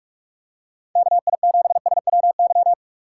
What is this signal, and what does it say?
Morse code, heard as a keyed tone, about 43 wpm, 700 Hz. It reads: KI7SWY